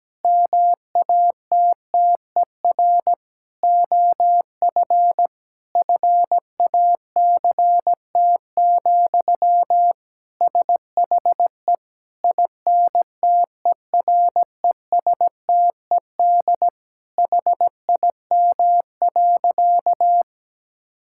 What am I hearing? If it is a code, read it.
Morse code, heard as a keyed tone, about 17 wpm, 700 Hz. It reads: MATTER OF FACT, SHE INTERESTED HIM.
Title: Sentences from Top 1000 Words - Review Part 6 - 17wpm